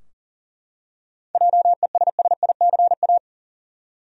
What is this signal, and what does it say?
Morse code, heard as a keyed tone, about 40 wpm, 700 Hz. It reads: JESSICA
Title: USA Common Names - 40wpm